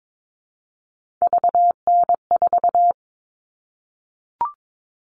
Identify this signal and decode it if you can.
Morse code, heard as a keyed tone, about 22 wpm, 700 Hz. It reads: VN4
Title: All Cut Numbers - 3-Digit 22wpm